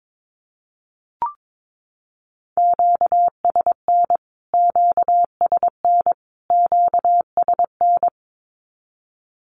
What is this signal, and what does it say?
Morse code, heard as a keyed tone, about 22 wpm, 700 Hz. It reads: QSN QSN QSN
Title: Q-Codes Repeated 3x 22wpm